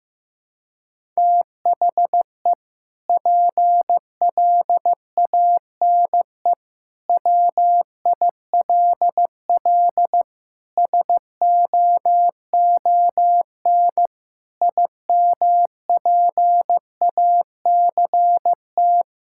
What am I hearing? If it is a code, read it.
Morse code, heard as a keyed tone, about 16 wpm, 700 Hz. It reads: THE PLANE WILL SOON IMPACT
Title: Sentences from Top 1000 Words - Review Part 6 - 15wpm